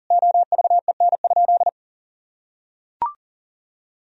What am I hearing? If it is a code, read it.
Morse code, heard as a keyed tone, about 40 wpm, 700 Hz. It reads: OVEN?
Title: Sentences from Top 1000 Words - Review Part 4 - 40wpm